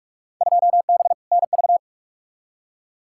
Morse code, heard as a keyed tone, about 47 wpm, 700 Hz. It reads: 2B NV
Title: Field Day Exchanges Encore - 45wpm